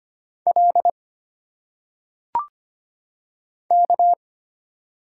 Morse code, heard as a keyed tone, about 25 wpm, 700 Hz. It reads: L K
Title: Single Letters - Phonetic Pronunciation 25wpm